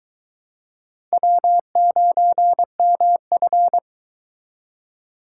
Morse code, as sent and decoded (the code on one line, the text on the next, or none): .-- ----. -- ..-.
W9MF